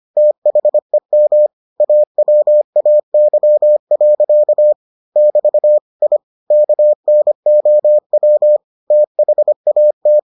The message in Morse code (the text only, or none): THEM AWAY. <BT> I KNOW THAT